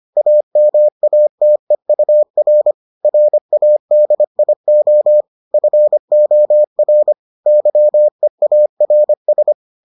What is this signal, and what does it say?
Morse code, heard as a keyed tone, about 25 wpm, 600 Hz. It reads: AMATEUR RADIO FOR YEARS